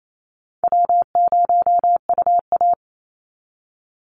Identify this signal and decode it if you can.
Morse code, heard as a keyed tone, about 28 wpm, 700 Hz. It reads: W0UA
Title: Easy Call Signs - 28wpm